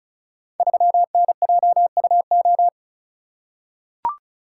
Morse code, heard as a keyed tone, about 36 wpm, 700 Hz. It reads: 3NJUO
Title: Code Groups - 5 Letter:Numbers - 35wpm